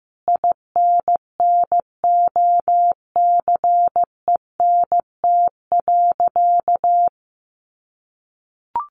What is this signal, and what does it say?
Morse code, heard as a keyed tone, about 15 wpm, 700 Hz. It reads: INNOCENT.
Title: A Sound Of Thunder 15wpm